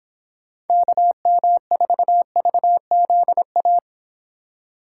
Morse code, heard as a keyed tone, about 27 wpm, 700 Hz. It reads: KM4VZA
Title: Speed Racing - USA 2x3 Call Signs 22wpm